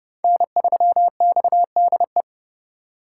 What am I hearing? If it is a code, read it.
Morse code, heard as a keyed tone, about 28 wpm, 700 Hz. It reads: N3XDE